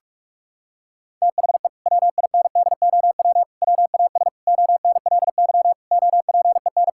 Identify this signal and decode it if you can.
Morse code, heard as a keyed tone, about 44 wpm, 700 Hz. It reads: THE WINDOW WAS ONLY OPEN
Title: Sentences from Top 1600 Words 45wpm